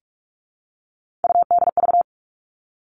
Morse code, heard as a keyed tone, about 44 wpm, 700 Hz. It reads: UDV
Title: All Cut Numbers - 3-Digit 45wpm